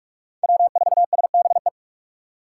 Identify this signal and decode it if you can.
Morse code, heard as a keyed tone, about 42 wpm, 700 Hz. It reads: W4SBE
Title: USA Call Signs - 5 Characters - 2x Character Spacing - 45wpm